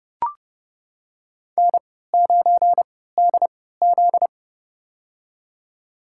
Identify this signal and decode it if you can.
Morse code, heard as a keyed tone, about 30 wpm, spaced as for 17 wpm, 700 Hz. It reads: N9DZ
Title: USA Call Signs - 4 Characters - 3x Character Spacing - 30wpm